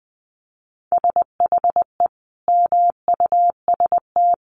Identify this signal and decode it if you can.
Morse code, heard as a keyed tone, about 20 wpm, 700 Hz. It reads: SHE MUST
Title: Sentences from Top 1400 Words 20wpm